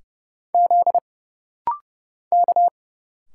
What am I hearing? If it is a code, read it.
Morse code, heard as a keyed tone, about 30 wpm, 700 Hz. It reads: Z K
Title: Single Letters - Rapid-Fire 30wpm